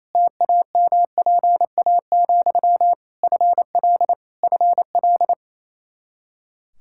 Morse code, heard as a keyed tone, about 28 wpm, 700 Hz. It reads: TAMPA, FL FL
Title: K1USN SST QSO Practice 28wpm